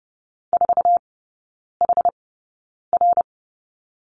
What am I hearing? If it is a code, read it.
Morse code, heard as a keyed tone, about 30 wpm, spaced as for 9 wpm, 700 Hz. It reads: 4HR
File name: course / 30:10wpm / 087 Lesson 15e - Number 5 - Three Characters 30wpm